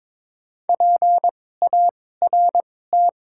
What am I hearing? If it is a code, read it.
Morse code, heard as a keyed tone, about 22 wpm, spaced as for 16 wpm, 700 Hz. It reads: PART
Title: 2x Character Spacing - Top 200 Words - 22wpm